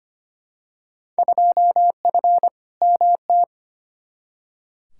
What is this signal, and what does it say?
Morse code, heard as a keyed tone, about 25 wpm, 700 Hz. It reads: 2F MT
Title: Field Day Exchanges Encore - 25wpm